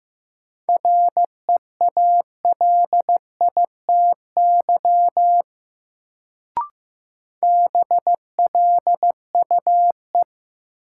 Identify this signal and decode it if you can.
Morse code, heard as a keyed tone, about 15 wpm, 700 Hz. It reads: REALITY BLUE
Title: Sentences from Top 1200 Words - New Word Drill Alt - 15wpm